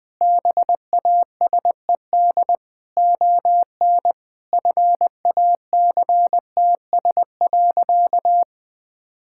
Morse code, heard as a keyed tone, about 20 wpm, 700 Hz. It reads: BASED ON FACTS.
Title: Sentences with 2-Word Phrases - Spoken Beginning - 20wpm